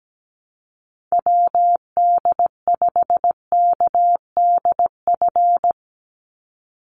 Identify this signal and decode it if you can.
Morse code, heard as a keyed tone, about 17 wpm, 700 Hz. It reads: WD5KDF
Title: USA Call Signs 17wpm